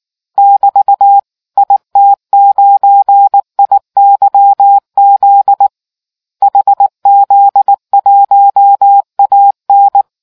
Morse code, heard as a keyed tone, about 19 wpm, 800 Hz. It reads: <BT> IT9IYZ HZ1AN